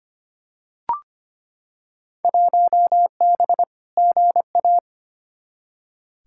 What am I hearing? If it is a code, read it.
Morse code, heard as a keyed tone, about 25 wpm, 700 Hz. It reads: 1B GA
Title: Field Day Exchanges - 25wpm